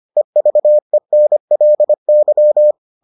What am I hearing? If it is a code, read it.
Morse code, heard as a keyed tone, about 25 wpm, 600 Hz. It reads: EVENLY